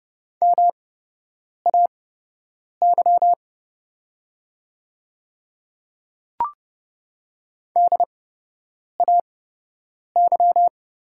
Morse code, heard as a keyed tone, about 30 wpm, spaced as for 8 wpm, 700 Hz. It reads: MAY DAY
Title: 8x Character Spacing - Top 100 Words - 30wpm